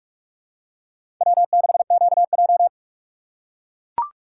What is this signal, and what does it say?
Morse code, heard as a keyed tone, about 44 wpm, 700 Hz. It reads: W6QJ
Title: USA Call Signs - 4 Characters - 6x Character Spacing - 45wpm